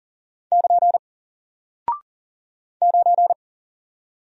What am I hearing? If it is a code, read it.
Morse code, heard as a keyed tone, about 40 wpm, 700 Hz. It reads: <KN> 9
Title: Single Character (Letter:Number:Punctuation:Prosign) Alt 40wpm